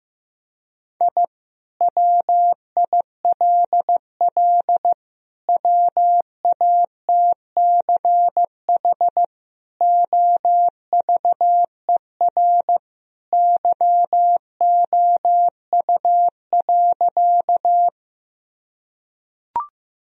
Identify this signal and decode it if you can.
Morse code, heard as a keyed tone, about 15 wpm, 700 Hz. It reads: I WILL WATCH OVER YOU.
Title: Sentences from Top 1000 Words - Review Part 1 - 15wpm